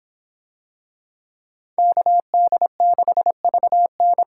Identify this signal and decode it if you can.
Morse code, heard as a keyed tone, about 24 wpm, 700 Hz. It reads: KD6VN